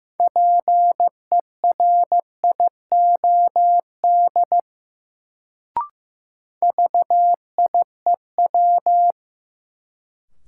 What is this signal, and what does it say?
Morse code, heard as a keyed tone, about 15 wpm, 700 Hz. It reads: PERIOD VIEW
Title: Sets of 5 Words - New Word Drill - 15wpm